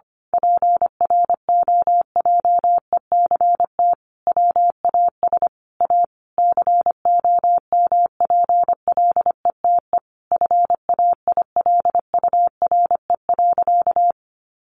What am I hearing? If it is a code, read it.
Morse code, heard as a keyed tone, about 25 wpm, 700 Hz. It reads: PROJECT WAS A COMPLETE FAILURE.
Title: Sentences from Top 1400 Words 25wpm